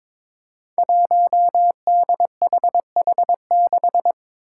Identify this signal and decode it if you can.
Morse code, heard as a keyed tone, about 22 wpm, 700 Hz. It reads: 1DHH6
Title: Five Challenging Letter-Number Combinations 22wpm